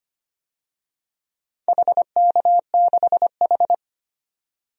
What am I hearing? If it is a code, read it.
Morse code, heard as a keyed tone, about 25 wpm, 690 Hz. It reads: HK6H
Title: DXCC-100 Call Signs - 4 Characters - 25wpm